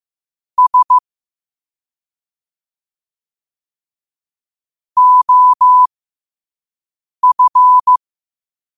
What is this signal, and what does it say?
Morse code, heard as a keyed tone, about 15 wpm, spaced as for 5 wpm, 1000 Hz. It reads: S OF
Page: New quotes every day in morse code at 5 Words per minute.